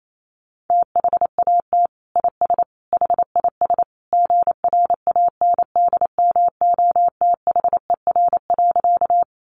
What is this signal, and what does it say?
Morse code, heard as a keyed tone, about 28 wpm, 700 Hz. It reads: THAT IS HIS GRANDMOTHER.
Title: Sentences from Top 1200 Words 28wpm